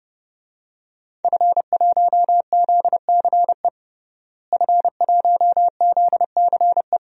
Morse code, heard as a keyed tone, about 30 wpm, 700 Hz. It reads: F1ZCE F1ZCE
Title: DXCC-100 Call Signs - 5 Characters - Repeated 2x - 30wpm